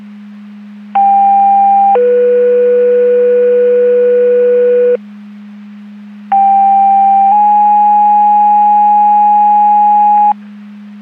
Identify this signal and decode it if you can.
Two-tone page, heard as a one-second tone, then a three-second tone, about 800 then 490 Hz, about 790 then 830 Hz. It is heard from 1.0 to 10.3 s.
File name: Portage 154.385 Paging